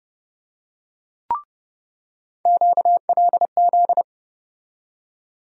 Morse code, heard as a keyed tone, about 30 wpm, 700 Hz. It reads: QLZ
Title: USA Call Sign Postfixes - 3 Letters 30wpm